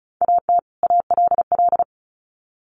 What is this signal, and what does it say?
Morse code, heard as a keyed tone, about 34 wpm, 700 Hz. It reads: AT ALL
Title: Sets of 2 Words 35wpm